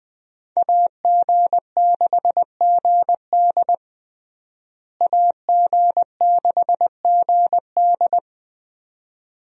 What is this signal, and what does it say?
Morse code, heard as a keyed tone, about 20 wpm, 700 Hz. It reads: AG6GD AG6GD
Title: USA Call Signs - 5 Characters - Repeated 2x - 20wpm